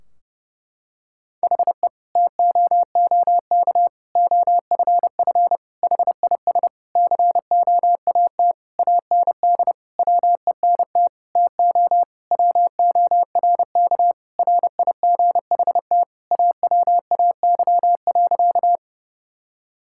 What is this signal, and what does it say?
Morse code, heard as a keyed tone, about 30 wpm, 700 Hz. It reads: HE TOOK OFF HIS COAT AND WENT TO WORK RIGHT AWAY.
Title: Sentences from Top 1600 Words 30wpm